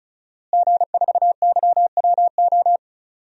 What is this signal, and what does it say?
Morse code, heard as a keyed tone, about 34 wpm, 700 Hz. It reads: G4YWO
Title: DXCC-100 Call Signs - 5 Characters - Repeated 3x - 35wpm